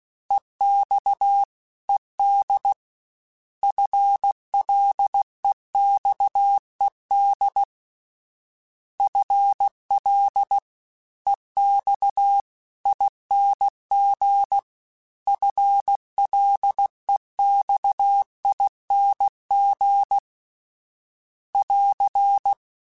Morse code, heard as a keyed tone, about 16 wpm, 800 Hz. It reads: EX ED FLEXED FL EX ING FLEXING <AR>
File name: FL / FL mp3s / Fast FL @15